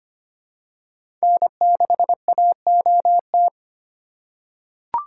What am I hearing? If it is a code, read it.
Morse code, heard as a keyed tone, about 25 wpm, 700 Hz. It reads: N6AOT
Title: USA Call Signs - 5 Characters - 2x Character Spacing - 25wpm